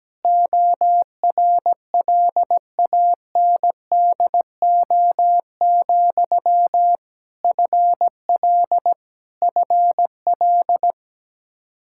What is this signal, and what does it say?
Morse code, heard as a keyed tone, about 17 wpm, 700 Hz. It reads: ORLANDO, FL FL
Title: Practice QSO - Exchange 1 - Beginning - 17wpm